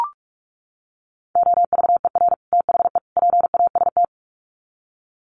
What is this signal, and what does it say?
Morse code, heard as a keyed tone, about 44 wpm, 700 Hz. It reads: OVER THE PAST
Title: Sets of 3 Words 45wpm